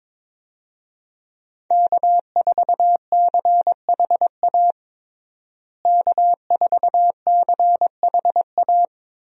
Morse code, heard as a keyed tone, about 22 wpm, 700 Hz. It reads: K4CHA K4CHA
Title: USA Call Signs - 5 Characters - Repeated 2x - 22wpm